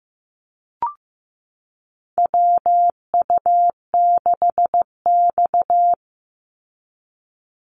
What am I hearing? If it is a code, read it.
Morse code, heard as a keyed tone, about 15 wpm, 700 Hz. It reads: WU6X
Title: Call Signs Repeated 2x - 15wpm